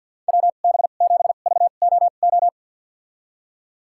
Morse code, heard as a keyed tone, about 47 wpm, spaced as for 35 wpm, 700 Hz. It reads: WB7VOO
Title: USA Call Signs - 2x Character Spacing - 50wpm